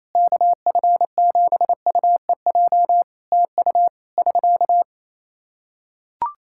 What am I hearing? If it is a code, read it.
Morse code, heard as a keyed tone, about 28 wpm, 700 Hz. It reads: KF7UEJ TU <SK>
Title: Practice QSO - Exchange 4 - Ending - 28wpm